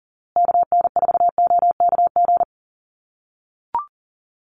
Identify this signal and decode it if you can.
Morse code, heard as a keyed tone, about 40 wpm, 700 Hz. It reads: KN4OKG
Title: USA Call Signs Repeated 3x 40wpm